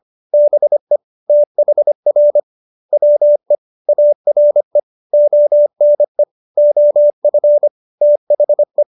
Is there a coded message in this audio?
Yes: Morse code from 0.3 to 8.8 s, heard as a keyed tone, about 25 wpm, 590 Hz. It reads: BE THR WE ARE ONE OF THE